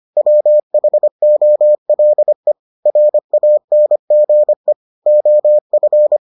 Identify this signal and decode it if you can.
Morse code, heard as a keyed tone, about 25 wpm, 600 Hz. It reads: WHOLE RANGE OF